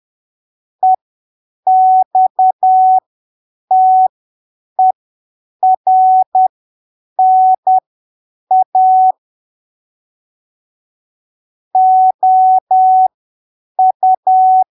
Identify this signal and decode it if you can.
Morse code, heard as a keyed tone, about 10 wpm, spaced as for 7 wpm, 750 Hz. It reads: EXTERNA OU
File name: arquivos / texto / texto 11 10ppm